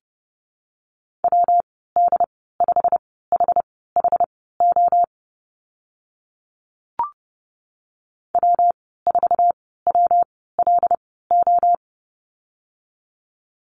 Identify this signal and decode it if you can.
Morse code, heard as a keyed tone, about 30 wpm, spaced as for 17 wpm, 700 Hz. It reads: WD5HHO W4WLO
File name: Lesson 19g - Letter w - Call Signs 30wpm